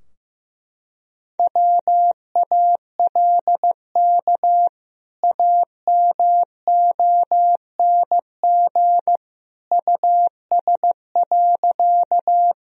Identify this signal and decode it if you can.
Morse code, heard as a keyed tone, about 16 wpm, 700 Hz. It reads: WALK AMONG US.
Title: Sentences from Top 1000 Words - Review Part 3 - 15wpm